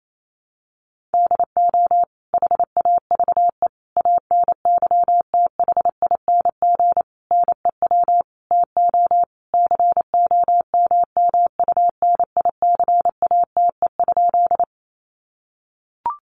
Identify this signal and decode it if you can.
Morse code, heard as a keyed tone, about 28 wpm, 700 Hz. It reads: DO HAVE ANYTHING NEW TO COMMUNICATE?
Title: Sentences from Top 1300 Words 28wpm